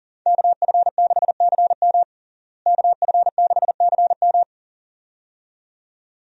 Morse code, heard as a keyed tone, about 38 wpm, 700 Hz. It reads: KF6CM KF6CM
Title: USA Call Signs - 5 Characters - Repeated 2x - 40wpm